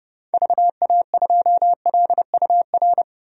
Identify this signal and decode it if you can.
Morse code, heard as a keyed tone, about 30 wpm, 700 Hz. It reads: VA2LUR
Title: DXCC-100 Call Signs - 6 Characters - 30wpm